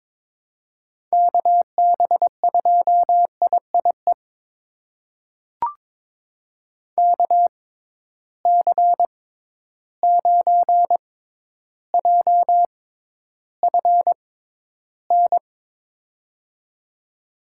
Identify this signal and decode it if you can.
Morse code, heard as a keyed tone, about 21 wpm, spaced as for 11 wpm, 700 Hz. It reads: KB2IIE KC9JFN
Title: USA Call Signs - 6 Characters - 6x Character Spacing - 22wpm